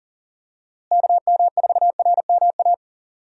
Morse code, heard as a keyed tone, about 40 wpm, 700 Hz. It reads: KM4RMA
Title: USA Call Signs - 6 Characters - Repeated 2x - 40wpm